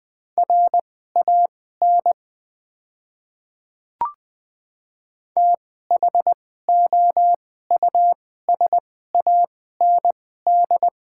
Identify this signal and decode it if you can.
Morse code, heard as a keyed tone, about 20 wpm, spaced as for 15 wpm, 690 Hz. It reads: RAN THOUSAND
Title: 2x Character Spacing - Top 500 Words - 20wpm